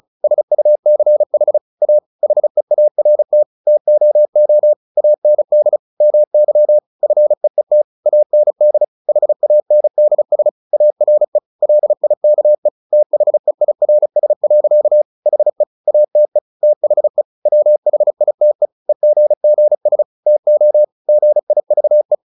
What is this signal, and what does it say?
Morse code, heard as a keyed tone, about 34 wpm, 600 Hz. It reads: SUCH A HEART TOO AND MY FEET AND HANDS ARE LIKE THEIRS. HE ATE THE WHITE EGGS TO GIVE
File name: the old man and the sea / the old man and the sea 1 35wpm